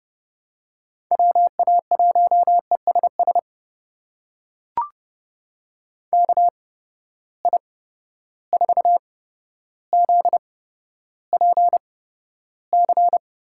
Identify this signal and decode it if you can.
Morse code, heard as a keyed tone, about 30 wpm, spaced as for 13 wpm, 700 Hz. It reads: WA1ESS KI4ZPC
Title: USA Call Signs - 6 Characters - 8x Character Spacing - 30wpm